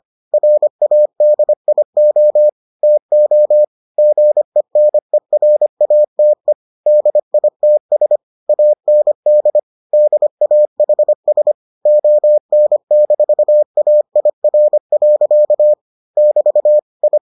Morse code, heard as a keyed tone, about 25 wpm, 600 Hz. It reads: RADIO TO GENERATE DITS AND DAHS ON-AIR. <BT> I